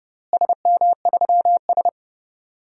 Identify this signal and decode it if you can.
Morse code, heard as a keyed tone, about 30 wpm, 700 Hz. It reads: SM3S
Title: DXCC-100 Call Signs - 4 Characters - Repeated 3x - 30wpm